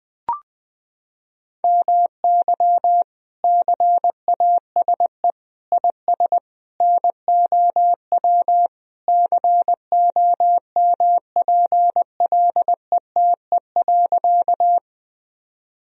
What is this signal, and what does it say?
Morse code, heard as a keyed tone, about 21 wpm, 700 Hz. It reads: MY CASE IS NOW COMPLETE.
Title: Sentences from Top 1000 Words - Review Part 2 - 20wpm